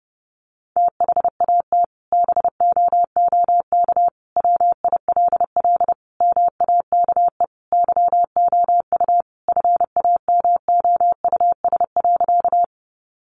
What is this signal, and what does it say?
Morse code, heard as a keyed tone, about 30 wpm, 700 Hz. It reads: THAT BOOK WILL MAKE YOU FAMOUS.
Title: Sentences from Top 1500 Words 30wpm